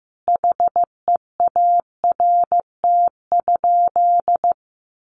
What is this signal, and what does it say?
Morse code, heard as a keyed tone, about 15 wpm, 700 Hz. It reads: HEART?
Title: Sentences from Top 1200 Words 15wpm